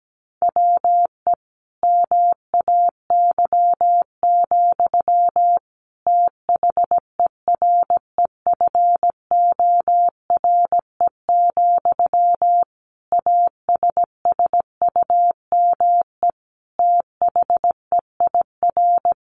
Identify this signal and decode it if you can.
Morse code, heard as a keyed tone, about 17 wpm, 700 Hz. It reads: WE MAY, THEREFORE, ASSUME THEIR
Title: Sentences from Top 1500 Words 17wpm